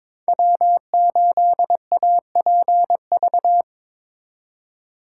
Morse code, heard as a keyed tone, about 26 wpm, 700 Hz. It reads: W8APV
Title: Speed Racing - USA 1x3 Call Signs 22wpm